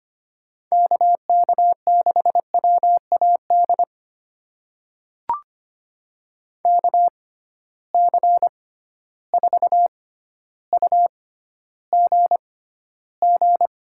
Morse code, heard as a keyed tone, about 25 wpm, spaced as for 13 wpm, 700 Hz. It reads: KK6WAD KC4UGG
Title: USA Call Signs - 6 Characters - 6x Character Spacing - 25wpm